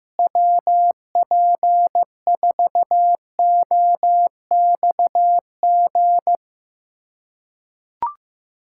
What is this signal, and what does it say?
Morse code, heard as a keyed tone, about 15 wpm, 700 Hz. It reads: WP4OXG
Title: USA Callsign DE Callsign - 6 Characters - 15wpm